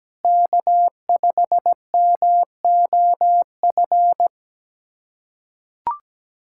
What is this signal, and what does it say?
Morse code, heard as a keyed tone, about 17 wpm, 700 Hz. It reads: K5MOF
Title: USA Callsign DE Callsign 17wpm